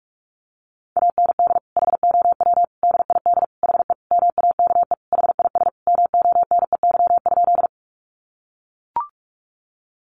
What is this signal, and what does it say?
Morse code, heard as a keyed tone, about 44 wpm, 700 Hz. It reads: AND HOW DID HE MAKE HIS MONEY?
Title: Sentences from Top 400 Words 45wpm